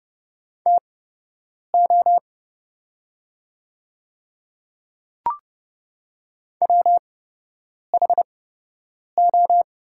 Morse code, heard as a keyed tone, about 30 wpm, spaced as for 8 wpm, 700 Hz. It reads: TO WHO
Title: 8x Character Spacing - Top 200 Words - 30wpm